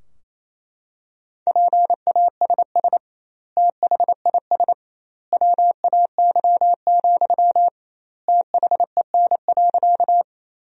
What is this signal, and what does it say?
Morse code, heard as a keyed tone, about 28 wpm, 700 Hz. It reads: PASS THIS WAY, THEN.
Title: 2x Word-Spacing - Sentences from Top 400 Words - 28wpm